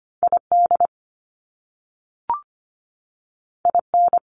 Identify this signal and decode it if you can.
Morse code, heard as a keyed tone, about 25 wpm, 700 Hz. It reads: ID IN
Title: US States 25wpm